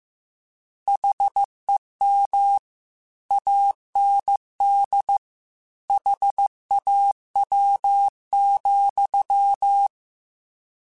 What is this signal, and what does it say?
Morse code, heard as a keyed tone, about 15 wpm, 800 Hz. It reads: HEM AND HAW,
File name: Misc / Misc-mp3s / Word Pairs @13